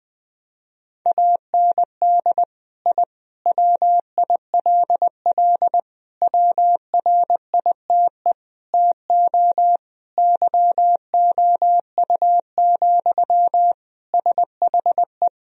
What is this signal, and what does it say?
Morse code, heard as a keyed tone, about 21 wpm, 700 Hz. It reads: AND I WILL WRITE TO YOU, SHE
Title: Sentences from Top 1000 Words - Review Part 1 - 20wpm